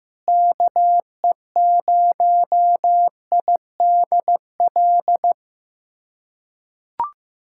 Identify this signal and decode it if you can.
Morse code, heard as a keyed tone, about 15 wpm, 700 Hz. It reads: KE0IDL
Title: USA Callsign DE Callsign - 6 Characters - 15wpm